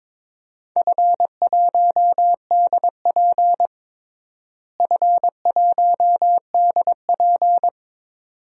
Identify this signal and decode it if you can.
Morse code, heard as a keyed tone, about 22 wpm, 700 Hz. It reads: F1DP F1DP
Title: DXCC-100 Call Signs - 4 Characters - Repeated 2x - 22wpm